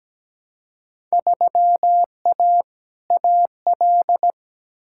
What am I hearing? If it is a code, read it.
Morse code, heard as a keyed tone, about 17 wpm, 700 Hz. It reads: 3A AL
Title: Field Day Exchanges - 17wpm